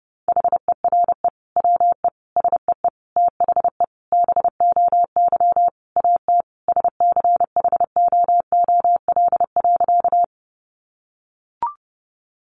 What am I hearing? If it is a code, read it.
Morse code, heard as a keyed tone, about 30 wpm, 700 Hz. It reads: HERE WE SEE THE BOY AT SCHOOL.
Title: Sentences from Top 300 Words 30wpm